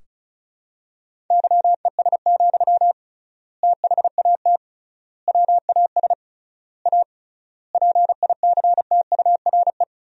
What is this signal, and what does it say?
Morse code, heard as a keyed tone, about 34 wpm, 700 Hz. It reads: YES, THAT WAS A PICTURE
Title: 3x Word-Spacing - Sentences from Top 500 Words - 35wpm